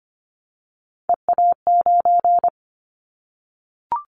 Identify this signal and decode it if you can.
Morse code, heard as a keyed tone, about 25 wpm, 700 Hz. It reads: EA9
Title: Callsign Prefixes for Top 100 DXCC - 25wpm